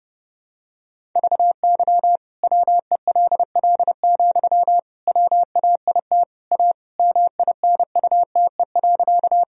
Morse code, heard as a keyed tone, about 32 wpm, 700 Hz. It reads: VY WELL, WAIT A MINUTE.
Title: Sentences from Top 500 Words and Abbreviations 30wpm